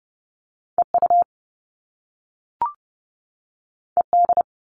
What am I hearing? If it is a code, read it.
Morse code, heard as a keyed tone, about 30 wpm, 700 Hz. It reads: EU ED
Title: All Cut Numbers - 2-Digit 30wpm